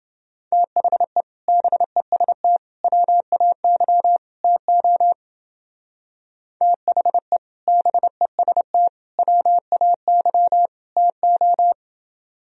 Morse code, heard as a keyed tone, about 29 wpm, 700 Hz. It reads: THE BEST WAY TO THE BEST WAY TO